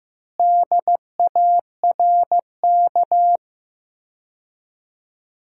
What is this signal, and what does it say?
Morse code, heard as a keyed tone, about 15 wpm, 700 Hz. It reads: DARK
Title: 4x Word-Spacing - Sentences from Top 500 Words - 15wpm